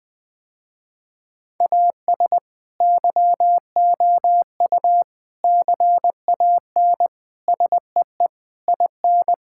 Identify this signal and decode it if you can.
Morse code, heard as a keyed tone, about 20 wpm, 700 Hz. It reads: AS YOU CAN SEE IN
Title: Sentences with 7-Word Phrases - Spoken Beginning - 20wpm